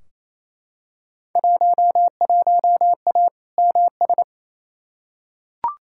Morse code, heard as a keyed tone, about 28 wpm, 700 Hz. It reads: 11A MS
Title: Field Day Exchanges Encore - 28wpm